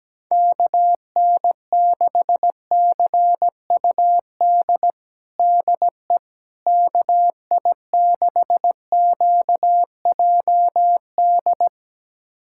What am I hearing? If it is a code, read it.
Morse code, heard as a keyed tone, about 17 wpm, 700 Hz. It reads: KN6CUD DE KI6QJD
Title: USA Callsign DE Callsign - 6 Characters - 17wpm